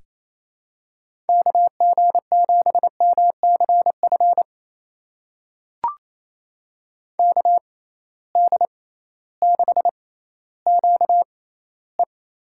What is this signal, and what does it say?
Morse code, heard as a keyed tone, about 28 wpm, spaced as for 15 wpm, 700 Hz. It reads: KG7MCF KD6QE
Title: USA Call Signs - 6x Character Spacing - 28wpm